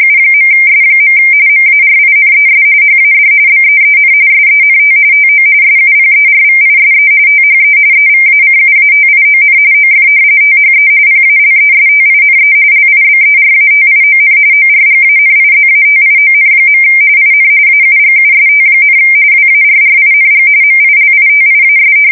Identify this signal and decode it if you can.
RTTY, about 45.45 baud. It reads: U.S. MARINES HAD BEEN SENT TO SUPERVISE THE EVACUATION OF THE PALESTINE LIBERATION ORGANIZATION. IN AUGUST, WARRING MILITIAS
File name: fsk170 / battleshipnj